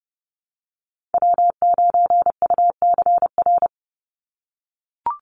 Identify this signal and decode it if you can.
Morse code, heard as a keyed tone, about 30 wpm, 700 Hz. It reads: W9UCR
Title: Call Signs - 30wpm